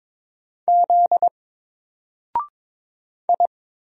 Morse code, heard as a keyed tone, about 22 wpm, 700 Hz. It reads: Z I
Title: Single Letter-Number Alt 22wpm